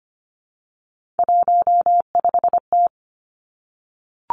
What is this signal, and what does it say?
Morse code, heard as a keyed tone, about 25 wpm, 700 Hz. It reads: 15T
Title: Three Challenging Letter-Number Combinations 25wpm